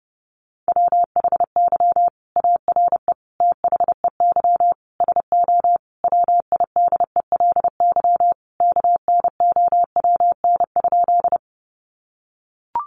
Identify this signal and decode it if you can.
Morse code, heard as a keyed tone, about 30 wpm, 700 Hz. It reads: WHY ARE THEY SO WIDELY KNOWN?
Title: Sentences from Top 1200 Words 30wpm